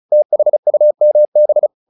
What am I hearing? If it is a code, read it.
Morse code, heard as a keyed tone, about 34 wpm, 600 Hz. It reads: THUMB
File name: the old man and the sea / the old man and the sea 1 35wpm